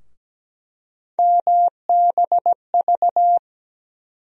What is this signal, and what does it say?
Morse code, heard as a keyed tone, about 17 wpm, 700 Hz. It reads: MBV
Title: USA Call Sign Postfixes - 3 Letters 17wpm